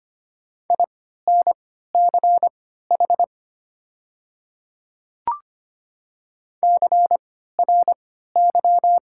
Morse code, heard as a keyed tone, about 25 wpm, spaced as for 14 wpm, 700 Hz. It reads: INCH CRY
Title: 3x Character Spacing - Top 500 Words - 25wpm